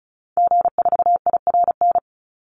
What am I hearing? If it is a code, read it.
Morse code, heard as a keyed tone, about 34 wpm, 700 Hz. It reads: G4IRN
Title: Call Signs Repeated 3x - 35wpm